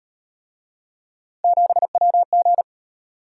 Morse code, heard as a keyed tone, about 38 wpm, 700 Hz. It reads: 7WG